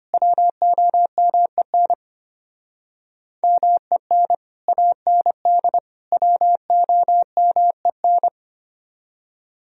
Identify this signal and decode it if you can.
Morse code, heard as a keyed tone, about 31 wpm, 700 Hz. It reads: WOMEN MEN AND WOMEN
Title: Speed-Racing - Sets of 3 Words 25wpm